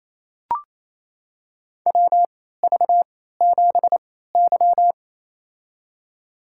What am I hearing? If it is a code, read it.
Morse code, heard as a keyed tone, about 28 wpm, spaced as for 16 wpm, 700 Hz. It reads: WV7Y
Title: USA Call Signs - 4 Characters - 3x Character Spacing - 28wpm